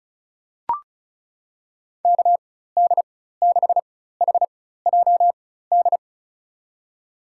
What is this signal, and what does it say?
Morse code, heard as a keyed tone, about 36 wpm, spaced as for 16 wpm, 700 Hz. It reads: KD6HJD
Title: USA Call Signs - 6 Characters - 4x Character Spacing - 35wpm